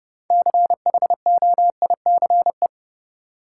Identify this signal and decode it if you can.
Morse code, heard as a keyed tone, about 30 wpm, 700 Hz. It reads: CHOICE
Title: Six Letter Words - 30wpm